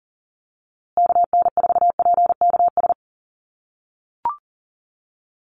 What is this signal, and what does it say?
Morse code, heard as a keyed tone, about 40 wpm, 700 Hz. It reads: KN4PKS
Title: USA Call Signs Repeated 2x 40wpm